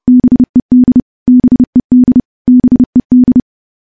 Morse code, heard as a keyed tone, about 30 wpm, 250 Hz. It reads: BED BED BED
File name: morse-training-30WPM